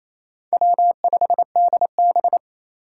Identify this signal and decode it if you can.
Morse code, heard as a keyed tone, about 28 wpm, 700 Hz. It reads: W5DB
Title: USA Call Signs - 4 Characters - 4x Character Spacing - 28wpm